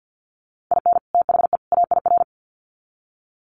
Morse code, heard as a keyed tone, about 48 wpm, 700 Hz. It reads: IN THE AIR
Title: Sets of 3 Words 50wpm